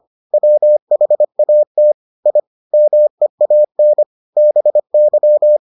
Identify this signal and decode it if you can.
Morse code, heard as a keyed tone, about 25 wpm, 600 Hz. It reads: WHAT I MEAN BY